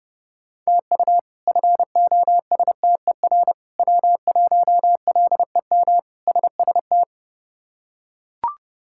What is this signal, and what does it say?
Morse code, heard as a keyed tone, about 30 wpm, 700 Hz. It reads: TU FOSTER W1LEM SST
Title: K1USN SST QSO Practice - 30wpm